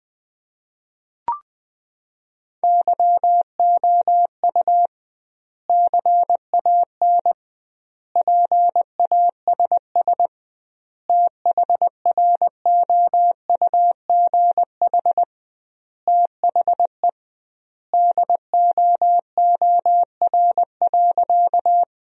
Morse code, heard as a keyed tone, about 20 wpm, 700 Hz. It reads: YOU CAN PASS THROUGH THE DOOR.
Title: 2x Word-Spacing - Sentences from Top 400 Words - 20wpm